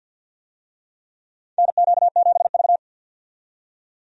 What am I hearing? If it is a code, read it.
Morse code, heard as a keyed tone, about 52 wpm, 700 Hz. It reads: NQ7V
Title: USA Call Signs - 4 Characters - 50wpm